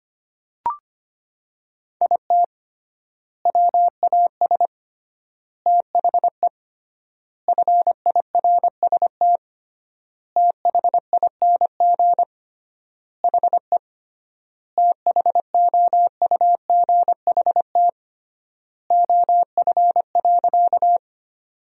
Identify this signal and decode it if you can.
Morse code, heard as a keyed tone, about 25 wpm, 700 Hz. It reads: IT WAS THE FIRST THING HE THOUGHT OF.
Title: 3x Word-Spacing - Sentences from Top 300 Words - 25wpm